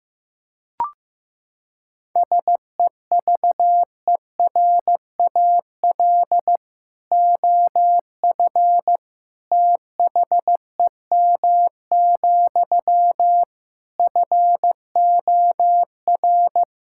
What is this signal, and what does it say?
Morse code, heard as a keyed tone, about 15 wpm, 700 Hz. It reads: SEVERAL OF THEM, FOR
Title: Sentences from Top 1000 Words - Review Part 5 - 15wpm